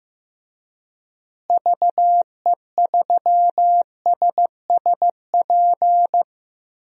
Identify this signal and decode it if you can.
Morse code, heard as a keyed tone, about 15 wpm, 700 Hz. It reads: VE3SSP
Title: DXCC-100 Call Signs - 6 Characters - Repeated 3x - 15wpm